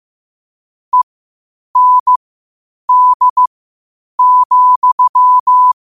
Morse code, heard as a keyed tone, about 15 wpm, spaced as for 8 wpm, 1000 Hz. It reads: END,
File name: qotdfiles / 8wpm